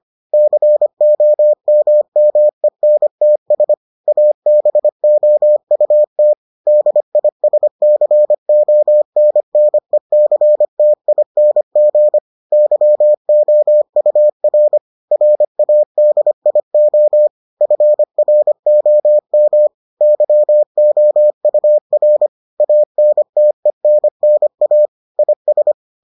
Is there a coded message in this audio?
Yes: Morse code from 0.3 to 25.7 s, heard as a keyed tone, about 25 wpm, 600 Hz. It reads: COMMENTS ABOUT DISCONNECTING YOUR RADIO FROM YOUR ANTENNA IS